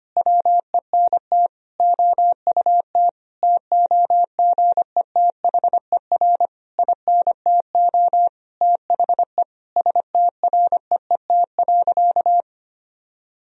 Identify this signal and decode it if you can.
Morse code, heard as a keyed tone, about 25 wpm, 700 Hz. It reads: WENT OUT TOGETHER INTO THE STREET.
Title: Sentences from Top 1000 Words - Review Part 3 - 25wpm